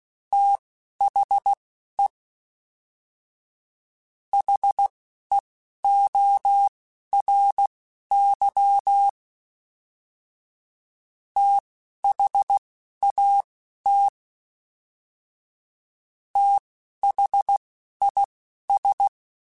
Morse code, heard as a keyed tone, about 16 wpm, spaced as for 11 wpm, 800 Hz. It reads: THE HEORY THAT THIS